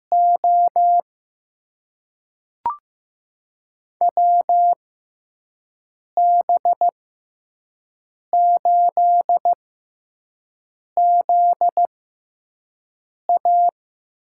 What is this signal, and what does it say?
Morse code, heard as a keyed tone, about 15 wpm, spaced as for 5 wpm, 700 Hz. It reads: O WB8ZA
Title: USA Call Signs - 6 Characters - 6x Character Spacing - 15wpm